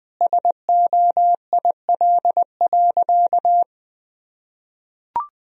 Morse code, heard as a keyed tone, about 20 wpm, 700 Hz. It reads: SOIL.
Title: Sentences from Top 1000 Words - Review Part 4 - 20wpm